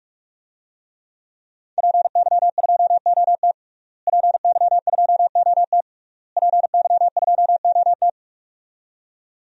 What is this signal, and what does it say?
Morse code, heard as a keyed tone, about 41 wpm, 700 Hz. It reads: PY2OT PY2OT PY2OT
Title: DXCC-100 Call Signs - 5 Characters - Repeated 3x - 45wpm